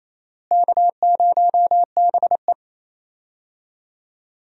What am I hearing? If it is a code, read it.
Morse code, heard as a keyed tone, about 28 wpm, 700 Hz. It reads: K0BE
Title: USA Call Signs - 4 Characters - Repeated 3x - 28wpm